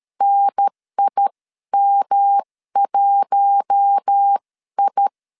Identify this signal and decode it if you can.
Morse code, heard as a keyed tone, about 13 wpm, spaced as for 11 wpm, 800 Hz. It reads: NIM1I
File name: CW / DET / 1 6 1500C S A 12wpm 3440